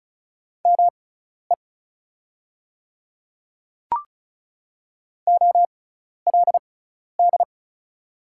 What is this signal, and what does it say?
Morse code, heard as a keyed tone, about 36 wpm, spaced as for 12 wpm, 690 Hz. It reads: ME OLD
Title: 6x Character Spacing - Top 200 Words - 35wpm